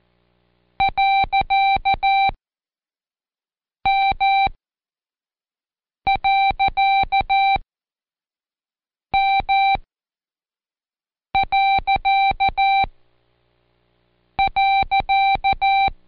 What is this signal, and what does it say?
Morse code, heard as a keyed tone, about 14 wpm, spaced as for 5 wpm, 780 Hz. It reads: .M.M..
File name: Code Quick / AudioDownld1 / Seg63-Day7-Pract PeriodMV